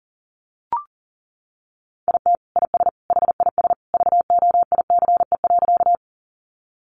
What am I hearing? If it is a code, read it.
Morse code, heard as a keyed tone, about 40 wpm, 700 Hz. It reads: IT IS HIS VOICE.
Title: Sentences from Top 500 Words 40wpm